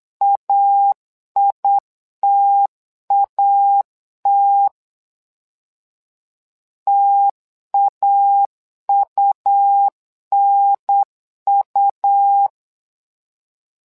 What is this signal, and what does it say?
Morse code, heard as a keyed tone, about 8 wpm, 800 Hz. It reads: AITAT TAUNU
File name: LEZIONI / 2 UHDN / 2 UHDN 8wpm2232